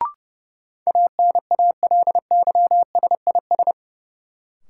Morse code, heard as a keyed tone, about 30 wpm, 700 Hz. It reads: ANALYSIS
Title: Sentences from Top 1400 Words - New Word Drill Alt - 30wpm